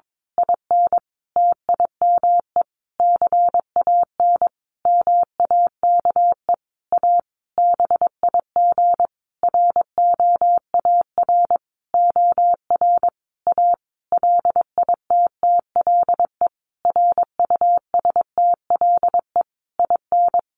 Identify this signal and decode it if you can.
Morse code, heard as a keyed tone, about 22 wpm, 700 Hz. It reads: IN TIME CAN MAKE A BIG ROAR OR A LITTLE RUSTLE IN
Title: A Sound Of Thunder 22wpm